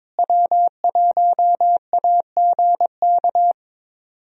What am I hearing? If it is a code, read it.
Morse code, heard as a keyed tone, about 22 wpm, 700 Hz. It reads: W1AGK
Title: USA Call Signs - 5 Characters - Repeated 2x - 22wpm